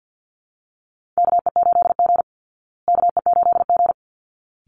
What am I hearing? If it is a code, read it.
Morse code, heard as a keyed tone, about 48 wpm, 700 Hz. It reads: KE8G KE8G
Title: Call Signs Repeated 3x - 50wpm